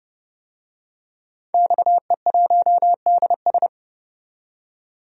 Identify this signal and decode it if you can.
Morse code, heard as a keyed tone, about 31 wpm, 700 Hz. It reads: XE1DS
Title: DXCC-100 Call Signs - 5 Characters - 30wpm